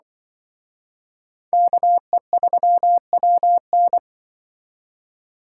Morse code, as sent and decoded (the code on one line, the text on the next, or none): -.- . ...-- .-- -.
KE3WN